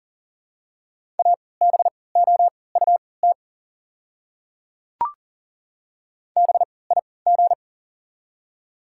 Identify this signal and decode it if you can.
Morse code, heard as a keyed tone, about 40 wpm, spaced as for 23 wpm, 700 Hz. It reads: ABOUT BIG
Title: 3x Character Spacing - Top 200 Words - 40wpm